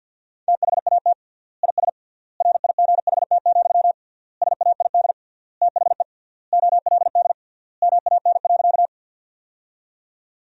Morse code, heard as a keyed tone, about 50 wpm, 700 Hz. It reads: THAT IS RIGHT, SAID THE OLD MAN.
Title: 3x Word-Spacing - Sentences from Top 200 Words - 50wpm